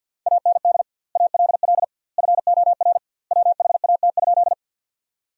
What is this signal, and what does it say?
Morse code, heard as a keyed tone, about 49 wpm, 700 Hz. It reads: AND ALL FOR WHAT?
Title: 2x Word-Spacing - Sentences from Top 100 Words - 50wpm